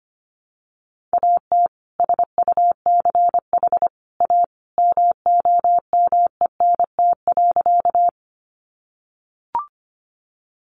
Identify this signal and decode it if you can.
Morse code, heard as a keyed tone, about 25 wpm, 700 Hz. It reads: AT SUCH A MOMENT.
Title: Sentences from Top 1000 Words 25wpm